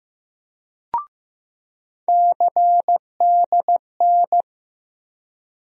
Morse code, heard as a keyed tone, about 15 wpm, 700 Hz. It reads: CDN
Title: USA Call Sign Postfixes - 3 Letters 15wpm